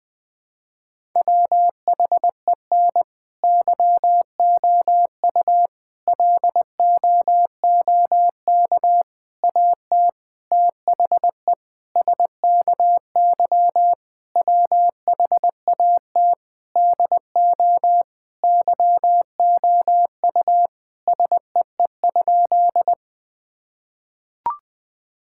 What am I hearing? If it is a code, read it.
Morse code, heard as a keyed tone, about 20 wpm, 700 Hz. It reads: WHEN YOU LOOK AT THE SKY WHAT DO YOU SEE?
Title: Sentences with 5-Word Phrases - 20wpm